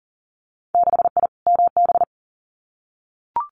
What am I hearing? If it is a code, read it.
Morse code, heard as a keyed tone, about 42 wpm, 700 Hz. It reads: 6I MB
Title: Winter Field Day Exchanges Encore 40wpm